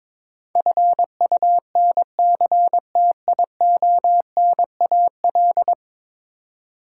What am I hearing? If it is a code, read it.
Morse code, heard as a keyed tone, about 22 wpm, 700 Hz. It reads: FUNCTIONAL
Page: morse-code-ninja-practice-22wpm